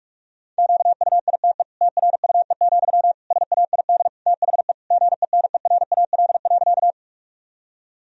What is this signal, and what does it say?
Morse code, heard as a keyed tone, about 41 wpm, 700 Hz. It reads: QUITE TRUE, SAID THE GENERAL.
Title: Sentences from Top 1000 Words - Review Part 3 - 45wpm